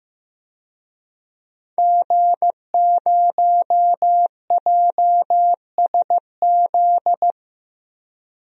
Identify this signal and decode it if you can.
Morse code, heard as a keyed tone, about 15 wpm, 700 Hz. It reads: G0JSZ
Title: DXCC-100 Call Signs - 5 Characters - Repeated 2x - 15wpm